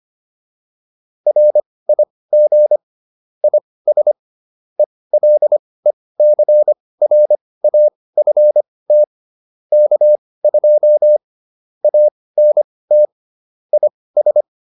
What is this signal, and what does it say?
Morse code, heard as a keyed tone, about 25 wpm, spaced as for 18 wpm, 600 Hz. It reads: RIG IS ELECRAFT K2 ANT IS